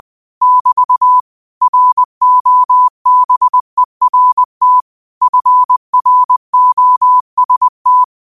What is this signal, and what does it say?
Morse code, heard as a keyed tone, about 20 wpm, 1000 Hz. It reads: <BT> ROBERT FROST
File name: qotdfiles / 20wpm